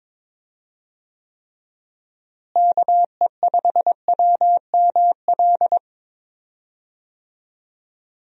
Morse code, heard as a keyed tone, about 22 wpm, 700 Hz. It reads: KE5WML
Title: USA Call Signs - 6 Characters - Repeated 2x - 22wpm